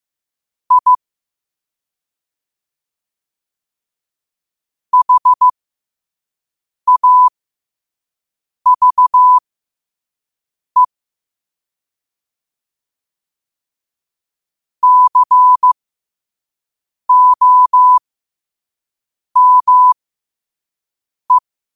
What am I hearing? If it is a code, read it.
Morse code, heard as a keyed tone, about 15 wpm, spaced as for 5 wpm, 1000 Hz. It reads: I HAVE COME
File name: qotdfiles / 5wpm